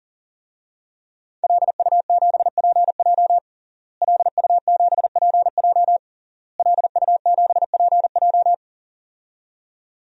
Morse code, heard as a keyed tone, about 40 wpm, 700 Hz. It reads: LU7PJ LU7PJ LU7PJ
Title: DXCC-100 Call Signs - 5 Characters - Repeated 3x - 40wpm